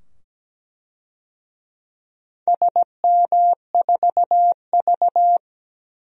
Morse code, heard as a keyed tone, about 17 wpm, 700 Hz. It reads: SM4V
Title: DXCC-100 Call Signs - 4 Characters - 17wpm